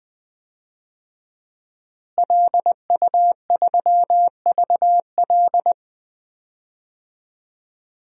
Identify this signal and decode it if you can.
Morse code, heard as a keyed tone, about 21 wpm, 700 Hz. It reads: LU3VL
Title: DXCC-100 Call Signs - 5 Characters - Repeated 2x - 20wpm